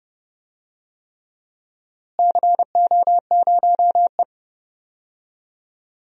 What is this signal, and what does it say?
Morse code, heard as a keyed tone, about 30 wpm, 700 Hz. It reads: CO0E
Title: DXCC-100 Call Signs - 4 Characters - Repeated 3x - 30wpm